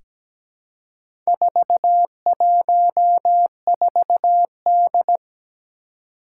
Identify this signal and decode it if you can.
Morse code, heard as a keyed tone, about 17 wpm, 700 Hz. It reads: 414D
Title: Four Challenging Letter-Number Combinations 17wpm